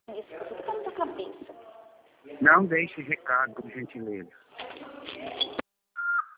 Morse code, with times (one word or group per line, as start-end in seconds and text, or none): none